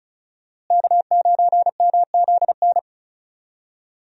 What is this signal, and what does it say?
Morse code, heard as a keyed tone, about 36 wpm, 700 Hz. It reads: K9MZN
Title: USA Call Signs - 5 Characters - Repeated 3x - 35wpm